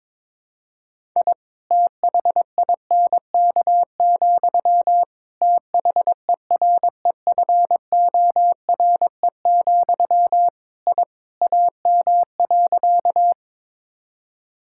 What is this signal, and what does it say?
Morse code, heard as a keyed tone, about 22 wpm, 700 Hz. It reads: I THINK, THEREFORE, I AM.
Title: Sentences from Top 1000 Words - Review Part 5 - 22wpm